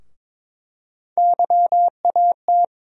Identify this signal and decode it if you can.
Morse code, heard as a keyed tone, about 23 wpm, 700 Hz. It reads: YAT
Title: USA Call Sign Postfixes - 3 Letters 22wpm